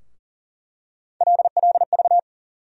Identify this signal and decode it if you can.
Morse code, heard as a keyed tone, about 39 wpm, 700 Hz. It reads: LLV
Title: USA Call Sign Postfixes - 3 Letters 40wpm